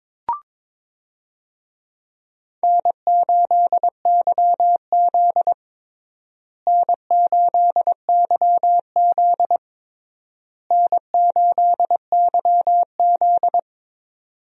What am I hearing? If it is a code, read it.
Morse code, heard as a keyed tone, about 22 wpm, 700 Hz. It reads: N8YZ N8YZ N8YZ
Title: USA Call Signs - 4 Characters - Repeated 3x - 22wpm